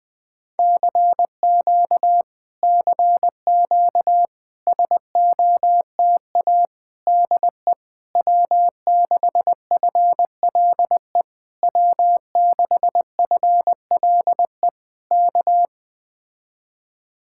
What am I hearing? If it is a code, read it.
Morse code, heard as a keyed tone, about 20 wpm, 700 Hz. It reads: CQ CQ SOTA DE W6FLE W6FLE K
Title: Speed-Racing Common QSO Phrases - 17wpm